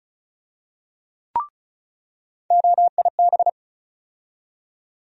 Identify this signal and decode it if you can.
Morse code, heard as a keyed tone, about 36 wpm, 700 Hz. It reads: OIB
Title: USA Call Sign Postfixes - 3 Letters 35wpm